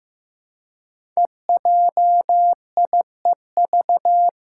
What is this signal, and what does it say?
Morse code, heard as a keyed tone, about 15 wpm, 700 Hz. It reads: EJIEV
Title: Five Challenging Letter-Number Combinations 15wpm